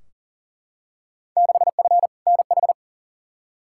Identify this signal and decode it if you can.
Morse code, heard as a keyed tone, about 40 wpm, 700 Hz. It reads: 6F NH
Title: Field Day Exchanges Encore - 40wpm